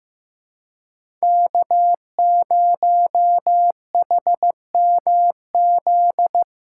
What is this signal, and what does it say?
Morse code, heard as a keyed tone, about 15 wpm, 700 Hz. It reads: K0HMZ
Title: K1USN SST QSO Practice - 15wpm